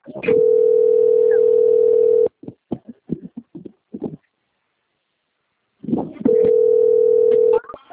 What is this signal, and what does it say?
Ringback tone (North America), 440 + 480 Hz.